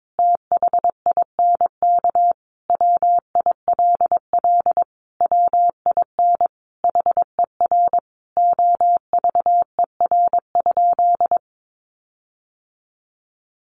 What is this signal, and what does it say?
Morse code, heard as a keyed tone, about 22 wpm, 700 Hz. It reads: THINK WILL WIN HER OVER?
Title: Sentences from Top 1500 Words 22wpm